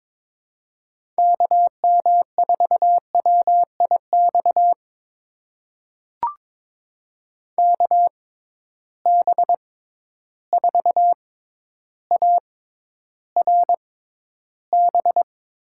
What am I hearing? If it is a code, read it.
Morse code, heard as a keyed tone, about 22 wpm, spaced as for 11 wpm, 700 Hz. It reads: KM4WIX KB4ARB
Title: USA Call Signs - 6 Characters - 6x Character Spacing - 22wpm